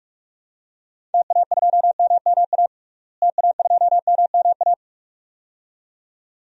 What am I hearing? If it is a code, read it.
Morse code, heard as a keyed tone, about 46 wpm, 700 Hz. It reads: TA2MMA TA2MMA
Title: DXCC-100 Call Signs - 6 Characters - Repeated 2x - 45wpm